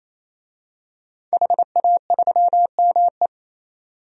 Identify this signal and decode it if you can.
Morse code, heard as a keyed tone, about 27 wpm, 700 Hz. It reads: HA3ME
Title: DXCC-100 Call Signs - 5 Characters - 28wpm